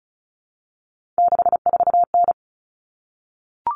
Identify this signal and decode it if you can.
Morse code, heard as a keyed tone, about 34 wpm, 700 Hz. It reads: 64N
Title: Common Cut Numbers - 3-Digit 35wpm